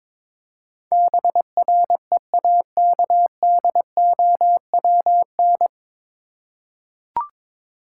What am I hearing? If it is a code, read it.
Morse code, heard as a keyed tone, about 22 wpm, 700 Hz. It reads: BREAKDOWN
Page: Morse Code Ninja practice 22wpm